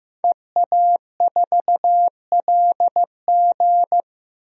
Morse code, heard as a keyed tone, about 15 wpm, 700 Hz. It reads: EA4LG
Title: DXCC-100 Call Signs - 5 Characters - Repeated 3x - 15wpm